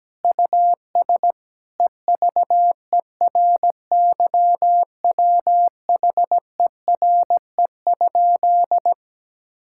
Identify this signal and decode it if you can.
Morse code, heard as a keyed tone, about 17 wpm, 700 Hz. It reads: US EVERYWHERE?
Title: Sentences from Top 1600 Words 17wpm